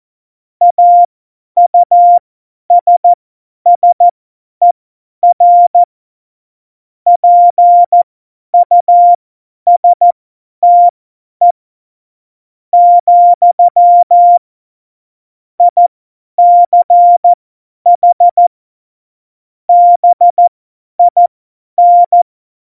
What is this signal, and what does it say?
Morse code, heard as a keyed tone, about 14 wpm, spaced as for 10 wpm, 700 Hz. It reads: AUSSER PUSTE, ICH BIN